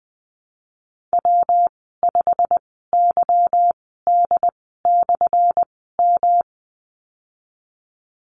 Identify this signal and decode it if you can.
Morse code, heard as a keyed tone, about 20 wpm, spaced as for 15 wpm, 700 Hz. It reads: W5YD/M
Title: 179 Lesson 29g - Slash - Call Signs 20wpm